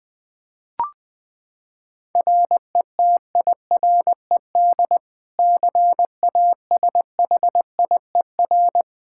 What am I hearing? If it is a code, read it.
Morse code, heard as a keyed tone, about 21 wpm, 700 Hz. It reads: RETIRED CASHIER
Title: Practice QSO - Exchange 3 - More Info - 20wpm